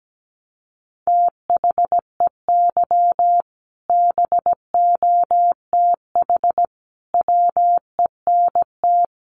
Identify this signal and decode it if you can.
Morse code, heard as a keyed tone, about 17 wpm, 700 Hz. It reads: THEY BOTH WENT
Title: Sentences from Top 600 Words 17wpm